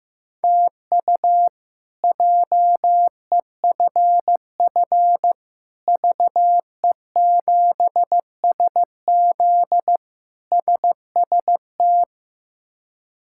Morse code, heard as a keyed tone, about 15 wpm, 700 Hz. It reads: TU JEFF VE7SZ SST
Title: K1USN SST QSO Practice - 15wpm